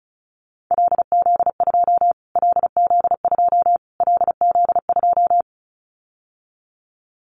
Morse code, heard as a keyed tone, about 34 wpm, 700 Hz. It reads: LZ2 LZ2 LZ2
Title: Callsign Prefixes for Top 25 DXCC - Repeated 3x - 35wpm